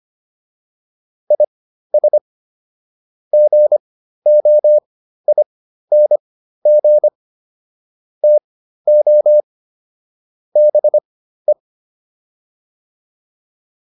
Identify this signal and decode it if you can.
Morse code, heard as a keyed tone, about 25 wpm, spaced as for 13 wpm, 600 Hz. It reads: IS GOING TO BE